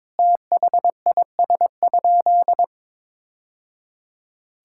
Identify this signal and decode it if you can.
Morse code, heard as a keyed tone, about 21 wpm, 700 Hz. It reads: THIS?
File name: practice / 22wpm / Sentences with 2-Word Phrases - 22wpm